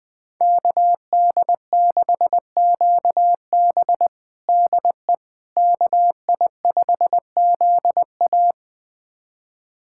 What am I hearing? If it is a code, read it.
Morse code, heard as a keyed tone, about 21 wpm, 700 Hz. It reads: KD6QB DE KI5ZA
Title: USA Callsign DE Callsign - 5 Characters - 20wpm